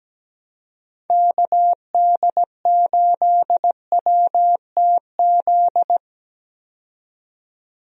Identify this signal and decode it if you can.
Morse code, heard as a keyed tone, about 17 wpm, 700 Hz. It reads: KD8WTZ
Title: USA Call Signs - 6 Characters - 17wpm